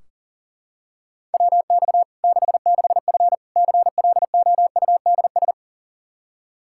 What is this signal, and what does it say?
Morse code, heard as a keyed tone, about 37 wpm, 700 Hz. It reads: WX 66F CLOUDS
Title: Practice QSO - Exchange 2 - Info - 40wpm